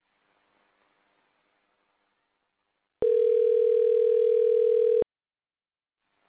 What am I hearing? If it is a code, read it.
Ringback tone (North America), 440 + 480 Hz.